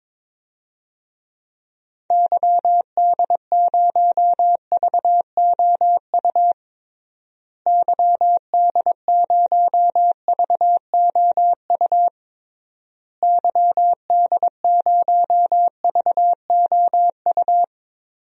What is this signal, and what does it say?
Morse code, heard as a keyed tone, about 22 wpm, 700 Hz. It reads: YD0VOU YD0VOU YD0VOU
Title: DXCC-100 Call Signs - 6 Characters - Repeated 3x - 22wpm